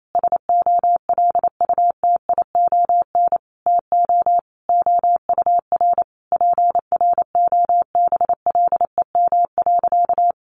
Morse code, heard as a keyed tone, about 28 wpm, 700 Hz. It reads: SOLUTION TO OUR PROBLEM.
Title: Sentences from Top 1300 Words 28wpm